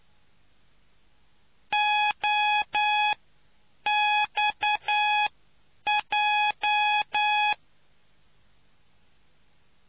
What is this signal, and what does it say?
Morse code, heard as a keyed tone, about 9 wpm, spaced as for 7 wpm, 840 Hz. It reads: OXJ